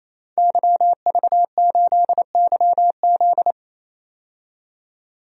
Morse code, heard as a keyed tone, about 27 wpm, 700 Hz. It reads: YV8YZ
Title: DXCC-100 Call Signs - 5 Characters - 28wpm